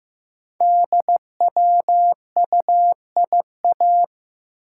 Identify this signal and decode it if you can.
Morse code, heard as a keyed tone, about 15 wpm, 700 Hz. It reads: DWUIA
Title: Code Groups - 5 Letter:Numbers - 15wpm